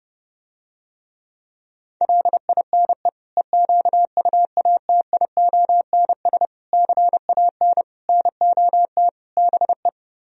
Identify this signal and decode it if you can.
Morse code, heard as a keyed tone, about 29 wpm, 700 Hz. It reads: LINE EQUATIONS CAN NOT BE